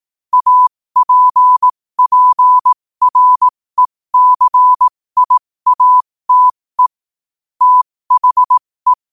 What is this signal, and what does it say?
Morse code, heard as a keyed tone, about 18 wpm, spaced as for 15 wpm, 1000 Hz. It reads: APPRECIATE THE
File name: qotdfiles / 15wpm